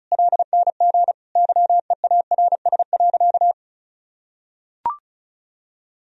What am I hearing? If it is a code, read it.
Morse code, heard as a keyed tone, about 34 wpm, 700 Hz. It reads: LNG YEARS.
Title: Sentences from Top 200 Words and Abbreviations 35wpm